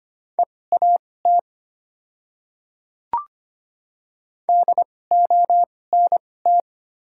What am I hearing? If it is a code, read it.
Morse code, heard as a keyed tone, about 25 wpm, spaced as for 18 wpm, 700 Hz. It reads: EAT DONT
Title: 2x Character Spacing - Top 300 Words - 25wpm